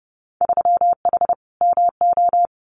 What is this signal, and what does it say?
Morse code, heard as a keyed tone, about 30 wpm, 700 Hz. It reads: 3H MO
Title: Winter Field Day Exchanges Encore 30wpm